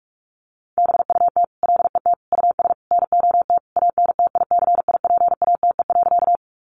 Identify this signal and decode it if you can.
Morse code, heard as a keyed tone, about 44 wpm, 700 Hz. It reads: BUT LET US NOT ANTICIPATE.
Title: Sentences from Top 1500 Words 45wpm